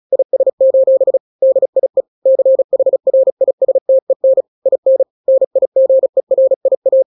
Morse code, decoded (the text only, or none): IS: DIE CHRISTEN IN NIGERIA